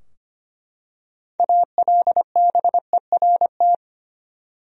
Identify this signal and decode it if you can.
Morse code, heard as a keyed tone, about 25 wpm, 700 Hz. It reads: ALBERT
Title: USA Common Names - 25wpm